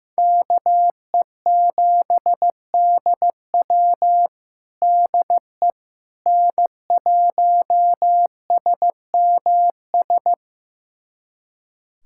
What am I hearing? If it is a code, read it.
Morse code, heard as a keyed tone, about 16 wpm, 700 Hz. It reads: KE7DW DE N1SMS
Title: USA Callsign DE Callsign - 5 Characters - 15wpm